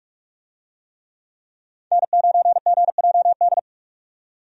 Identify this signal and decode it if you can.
Morse code, heard as a keyed tone, about 46 wpm, 700 Hz. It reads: N9GJD
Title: USA Call Signs - 5 Characters - 45wpm